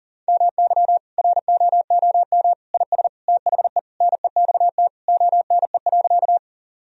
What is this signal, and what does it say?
Morse code, heard as a keyed tone, about 40 wpm, 700 Hz. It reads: MY ROOM IS THE NEXT ONE.
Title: Sentences from Top 1000 Words - Review Part 1 - 40wpm